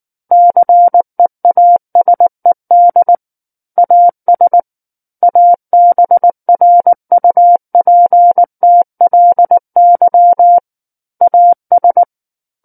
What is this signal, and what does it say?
Morse code, heard as a keyed tone, about 19 wpm, 700 Hz. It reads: CEASED AS ABRUPTLY AS
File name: Book One / 10-Chapter 10 - 19 WPM